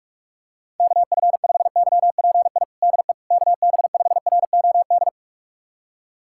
Morse code, heard as a keyed tone, about 46 wpm, 700 Hz. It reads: KF5YPI DE KB5ROD
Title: USA Callsign DE Callsign 45wpm